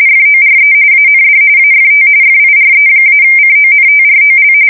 RTTY, about 45.45 baud. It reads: FORE AND AFT FIRE-CONTROL
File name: fsk170 / USS-Missouri